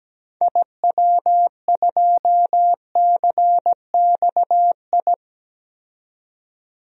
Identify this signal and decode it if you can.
Morse code, heard as a keyed tone, about 17 wpm, 700 Hz. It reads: IW2CXI
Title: DXCC-100 Call Signs - 6 Characters - 17wpm